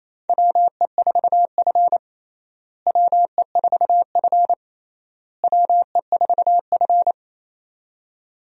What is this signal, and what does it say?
Morse code, heard as a keyed tone, about 28 wpm, 700 Hz. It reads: WE4F WE4F WE4F
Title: USA Call Signs - 4 Characters - Repeated 3x - 28wpm